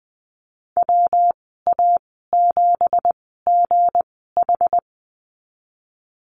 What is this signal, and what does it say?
Morse code, heard as a keyed tone, about 20 wpm, spaced as for 15 wpm, 700 Hz. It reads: WA7GH
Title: 219 Lesson 35g - Number 0 - Call Signs 20wpm